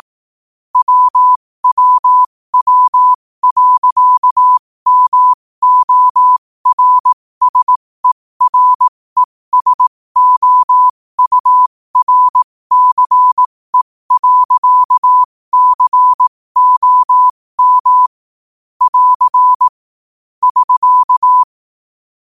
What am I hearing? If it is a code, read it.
Morse code, heard as a keyed tone, about 18 wpm, spaced as for 15 wpm, 1000 Hz. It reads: WWW.MORSERESOURCE.COM <AR> <SK>
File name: qotdfiles / 15wpm